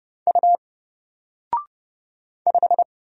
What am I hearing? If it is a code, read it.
Morse code, heard as a keyed tone, about 30 wpm, 700 Hz. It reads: U 5
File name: Single-Letter-Number-30wpm